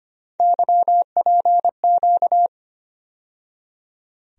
Morse code, heard as a keyed tone, about 25 wpm, 700 Hz. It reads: YPQ
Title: USA Call Sign Postfixes - 3 Letters 25wpm